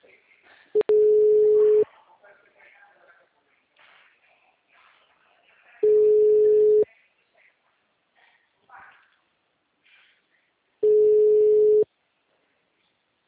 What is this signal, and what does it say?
Ringback tone (Europe, 425 Hz), with speech.
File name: out- -2014- - - .93307